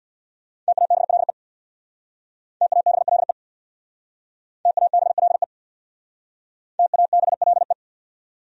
Morse code, heard as a keyed tone, about 60 wpm, 700 Hz. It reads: TABLE TABLE TABLE TABLE
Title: Speed-Racing Top 500 Words 50wpm